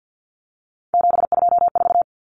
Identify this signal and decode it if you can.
Morse code, heard as a keyed tone, about 50 wpm, 700 Hz. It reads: 724
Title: Numbers - 3-Digit 50wpm